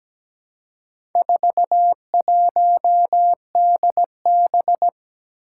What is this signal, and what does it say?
Morse code, heard as a keyed tone, about 17 wpm, 700 Hz. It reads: 41DB
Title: Four Challenging Letter-Number Combinations 17wpm